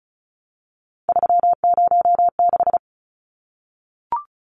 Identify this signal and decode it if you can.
Morse code, heard as a keyed tone, about 34 wpm, 700 Hz. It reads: 306
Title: Numbers - 3-Digit 35wpm